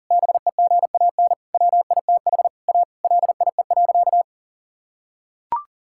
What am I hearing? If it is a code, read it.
Morse code, heard as a keyed tone, about 40 wpm, 700 Hz. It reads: BEGAN WITH A LIE.
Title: Sentences from Top 1600 Words 40wpm